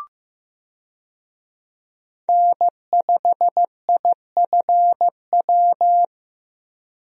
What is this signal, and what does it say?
Morse code, heard as a keyed tone, about 15 wpm, 700 Hz. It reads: N5IFW
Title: USA Call Signs - 5 Characters - Repeated 2x - 15wpm